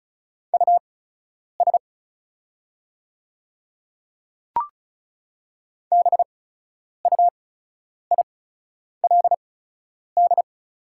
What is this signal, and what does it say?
Morse code, heard as a keyed tone, about 36 wpm, spaced as for 10 wpm, 700 Hz. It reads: US BUILD
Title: 8x Character Spacing - Top 200 Words - 35wpm